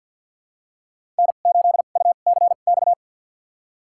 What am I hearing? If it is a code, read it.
Morse code, heard as a keyed tone, about 51 wpm, spaced as for 36 wpm, 700 Hz. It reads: N8UCX
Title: USA Call Signs - 5 Characters - 2x Character Spacing - 50wpm